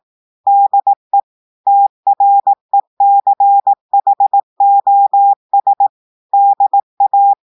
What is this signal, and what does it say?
Morse code, heard as a keyed tone, about 18 wpm, 800 Hz. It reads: DE TRECHOS DA
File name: audios / portugais / portugais texte en clair 1 vitesse 18 mots minutes